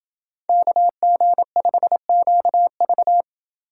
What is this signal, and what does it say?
Morse code, heard as a keyed tone, about 28 wpm, 700 Hz. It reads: KG5QV